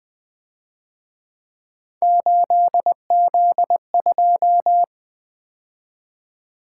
Morse code, heard as a keyed tone, about 25 wpm, 700 Hz. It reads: 8Z2
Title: Speed-Racing - Three Letter-Number Combinations - 20wpm